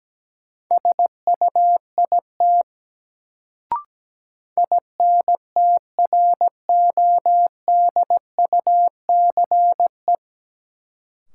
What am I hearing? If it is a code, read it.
Morse code, heard as a keyed tone, about 17 wpm, 700 Hz. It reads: SUIT INTRODUCE
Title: Sentences from Top 1600 Words - New Word Drill Alt - 17wpm